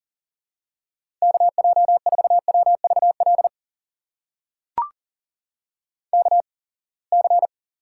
Morse code, heard as a keyed tone, about 40 wpm, spaced as for 27 wpm, 700 Hz. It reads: KJ4WVL KC
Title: USA Call Signs - 6 Characters - 8x Character Spacing - 40wpm